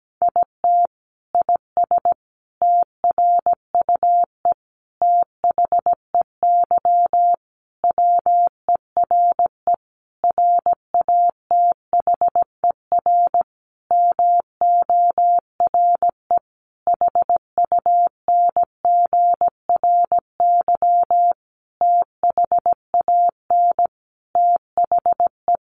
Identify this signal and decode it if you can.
Morse code, heard as a keyed tone, about 17 wpm, 700 Hz. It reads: IT IS TRUE THEY WERE RATHER MORE HUNGRY THAN THE
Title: Sentences from Top 1400 Words 17wpm